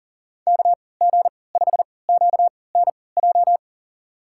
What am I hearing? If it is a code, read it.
Morse code, heard as a keyed tone, about 40 wpm, spaced as for 23 wpm, 700 Hz. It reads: KG5QNJ
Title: USA Call Signs - 3x Character Spacing - 40wpm